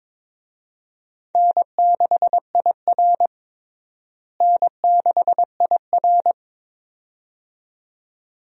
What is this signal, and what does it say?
Morse code, heard as a keyed tone, about 22 wpm, 700 Hz. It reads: N6IR N6IR
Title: USA Call Signs - 4 Characters - Repeated 2x - 22wpm